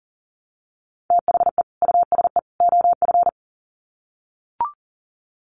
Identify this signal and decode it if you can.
Morse code, heard as a keyed tone, about 40 wpm, 700 Hz. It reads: THE USE OF
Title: Sets of 3 Words 40wpm